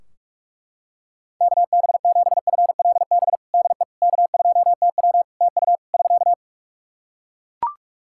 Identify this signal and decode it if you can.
Morse code, heard as a keyed tone, about 43 wpm, 700 Hz. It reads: KB7FLB DE K2TW TU <SK>
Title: Practice QSO - Exchange 4 - Ending - 45wpm